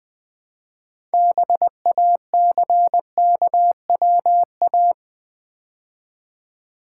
Morse code, heard as a keyed tone, about 20 wpm, 700 Hz. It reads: BACKWA
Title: Build A Word - 3 Letters At A Time - 9-Letter Words - 20wpm